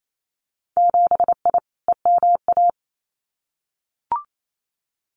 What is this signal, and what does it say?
Morse code, heard as a keyed tone, about 28 wpm, 700 Hz. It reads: 7I EMA
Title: Winter Field Day Exchanges Encore 28wpm